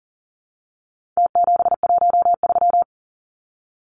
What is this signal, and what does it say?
Morse code, heard as a keyed tone, about 40 wpm, 700 Hz. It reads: T713
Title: Numbers - Contest Style - 4-Digit 40wpm